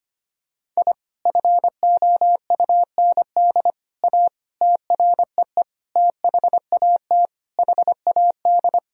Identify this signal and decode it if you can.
Morse code, heard as a keyed tone, about 25 wpm, 700 Hz. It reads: I FOUND A TREE THAT HAD
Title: Sentences from Top 1000 Words - Review Part 1 - 25wpm